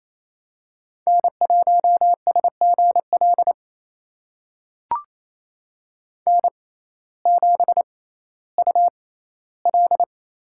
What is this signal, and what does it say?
Morse code, heard as a keyed tone, about 28 wpm, spaced as for 15 wpm, 700 Hz. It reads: N1SGL N7UL
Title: USA Call Signs - 5 Characters - 6x Character Spacing - 28wpm